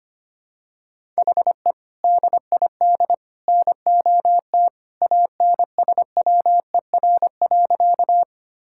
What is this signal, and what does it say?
Morse code, heard as a keyed tone, about 25 wpm, 700 Hz. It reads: HE DID NOT ANSWER.
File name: practice / 25wpm / Sentences with 3-Word Phrases - 25wpm